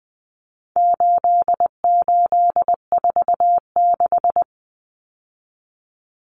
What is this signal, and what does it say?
Morse code, heard as a keyed tone, about 20 wpm, 700 Hz. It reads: 8846
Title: Numbers - 4-Digit 20wpm